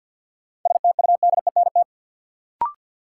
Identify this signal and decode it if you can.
Morse code, heard as a keyed tone, about 50 wpm, 700 Hz. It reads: STUDENT
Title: Sentences from Top 1000 Words - New Word Drill - 50wpm